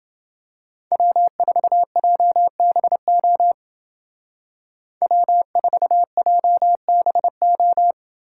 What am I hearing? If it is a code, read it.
Morse code, heard as a keyed tone, about 28 wpm, 700 Hz. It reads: W4JBO W4JBO